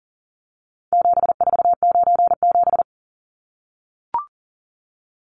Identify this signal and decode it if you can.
Morse code, heard as a keyed tone, about 40 wpm, 700 Hz. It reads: 7497
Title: Numbers - 4-Digit 40wpm